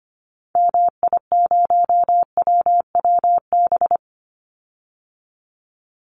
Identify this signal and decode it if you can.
Morse code, heard as a keyed tone, about 25 wpm, 700 Hz. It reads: MI0WWB
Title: Call Signs - 25wpm